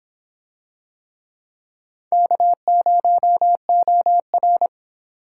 Morse code, heard as a keyed tone, about 26 wpm, 700 Hz. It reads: K0OR